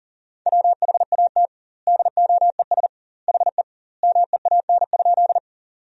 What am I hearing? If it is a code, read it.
Morse code, heard as a keyed tone, about 40 wpm, 700 Hz. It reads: WHAT DOES HE MEAN?
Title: 2x Word-Spacing - Sentences from Top 200 Words - 40wpm